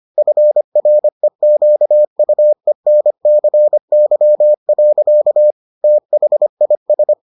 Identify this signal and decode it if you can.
Morse code, heard as a keyed tone, about 25 wpm, 600 Hz. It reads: FREQUENCY. THIS